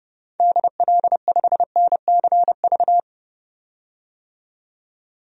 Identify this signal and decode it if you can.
Morse code, heard as a keyed tone, about 30 wpm, 700 Hz. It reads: DL5NCV
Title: DXCC-100 Call Signs - 6 Characters - 30wpm